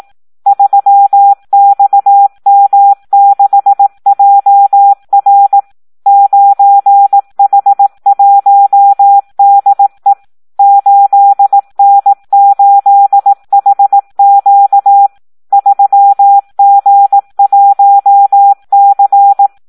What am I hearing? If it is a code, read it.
Morse code, heard as a keyed tone, about 18 wpm, 800 Hz. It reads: 3XM6JR 9H1DE 8N8HQ 3G1C
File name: Audio practica CW / SD 1 18